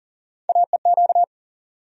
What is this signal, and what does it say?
Morse code, heard as a keyed tone, about 40 wpm, 700 Hz. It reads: AEQ
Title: USA Call Sign Postfixes - 3 Letters 40wpm